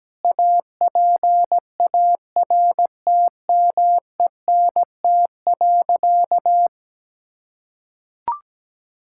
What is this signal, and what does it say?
Morse code, heard as a keyed tone, about 17 wpm, 700 Hz. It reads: APARTMENT.
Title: Sentences from Top 1000 Words - Review Part 6 - 17wpm